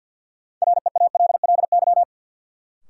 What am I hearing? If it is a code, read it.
Morse code, heard as a keyed tone, about 48 wpm, 700 Hz. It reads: REALLY
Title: Sentences from Top 600 Words - New Word Drill - 50wpm